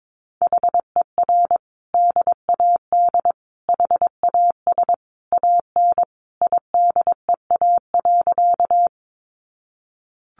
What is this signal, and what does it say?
Morse code, heard as a keyed tone, about 22 wpm, 700 Hz. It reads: HER DAD HAS AN IDEA.
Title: Easy Phrases 22wpm